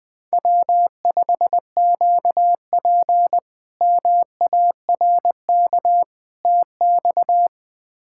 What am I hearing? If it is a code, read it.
Morse code, heard as a keyed tone, about 20 wpm, 700 Hz. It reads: W5QP MARK TX
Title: K1USN SST QSO Practice - 20wpm